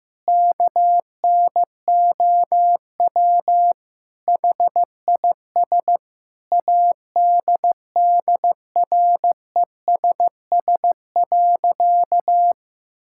Morse code, heard as a keyed tone, about 15 wpm, 700 Hz. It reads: KNOW HIS ADDRESS.
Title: Sentences from Top 1000 Words - Review Part 6 - 15wpm